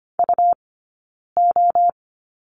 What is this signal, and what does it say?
Morse code, heard as a keyed tone, about 25 wpm, 700 Hz. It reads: U O
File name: Lesson 17d - Letter u - Two Characters 25wpm